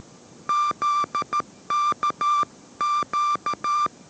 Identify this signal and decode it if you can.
Morse code, heard as a keyed tone, about 13 wpm, 1180 Hz. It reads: ZK Q
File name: 14wpm